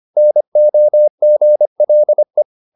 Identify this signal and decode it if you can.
Morse code, heard as a keyed tone, about 25 wpm, 600 Hz. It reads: NOGLE
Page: Ferie 25 wpm